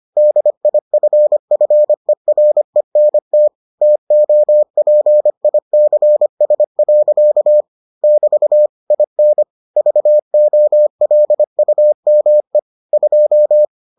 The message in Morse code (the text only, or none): DIFFERENT TOPICS. <BT> IN VOLUME 2